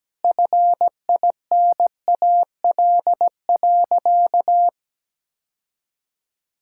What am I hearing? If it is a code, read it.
Morse code, heard as a keyed tone, about 17 wpm, 700 Hz. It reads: FINAL.
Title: Sentences from Top 1600 Words 17wpm